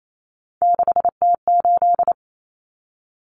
Morse code, heard as a keyed tone, about 28 wpm, 700 Hz. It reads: 6T8
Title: Common Cut Numbers - 3-Digit 28wpm